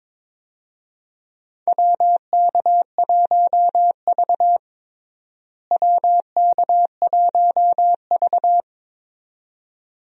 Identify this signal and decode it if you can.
Morse code, heard as a keyed tone, about 22 wpm, 700 Hz. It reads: WK1V WK1V
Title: USA Call Signs - 4 Characters - Repeated 2x - 22wpm